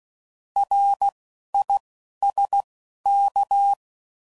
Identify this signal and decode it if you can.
Morse code, heard as a keyed tone, about 16 wpm, 800 Hz. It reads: R I S K
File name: Prefixes / By First Letter / Rs